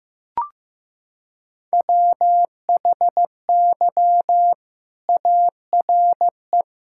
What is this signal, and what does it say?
Morse code, heard as a keyed tone, about 15 wpm, 700 Hz. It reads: WHY ARE
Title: Sentences from Top 1000 Words - Review Part 6 - 15wpm